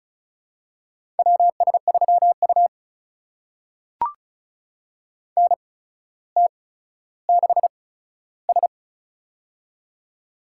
Morse code, heard as a keyed tone, about 36 wpm, spaced as for 15 wpm, 700 Hz. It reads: WS3U NT6S
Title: USA Call Signs - 4 Characters - 8x Character Spacing - 35wpm